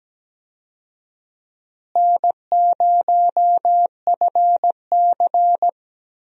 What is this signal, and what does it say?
Morse code, heard as a keyed tone, about 17 wpm, 700 Hz. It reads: N0FC
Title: USA Call Signs - 4 Characters - Repeated 2x - 17wpm